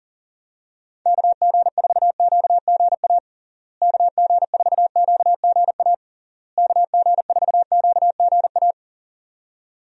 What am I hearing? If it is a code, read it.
Morse code, heard as a keyed tone, about 40 wpm, 700 Hz. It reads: KG4QGA KG4QGA KG4QGA
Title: USA Call Signs - 6 Characters - Repeated 3x - 40wpm